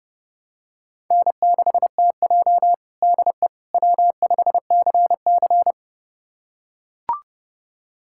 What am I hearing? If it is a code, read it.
Morse code, heard as a keyed tone, about 30 wpm, 700 Hz. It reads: N6TJ DE W5CC
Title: USA Callsign DE Callsign - 4 Characters - 30wpm